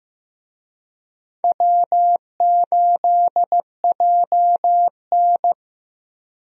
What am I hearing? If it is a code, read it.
Morse code, heard as a keyed tone, about 16 wpm, 700 Hz. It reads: W8JN
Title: USA Call Signs - 4 Characters - Repeated 2x - 15wpm